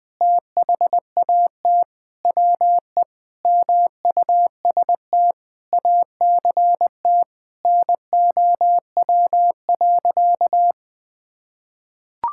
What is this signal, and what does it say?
Morse code, heard as a keyed tone, about 20 wpm, 700 Hz. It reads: THAT WE MUST ACT NOW.
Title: Sentences with 6-Word Phrases - Spoken Beginning - 20wpm